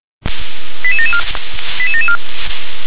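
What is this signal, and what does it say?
Five-tone selective call (ZVEI), heard as quick runs of five tones, tones of about 70 ms. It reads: 88774 88774
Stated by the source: ZVEI five-tone call received by SDR.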